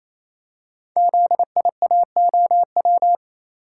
Morse code, heard as a keyed tone, about 29 wpm, 700 Hz. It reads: ZIAOW
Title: Code Groups - 5 Letter:Numbers - 28wpm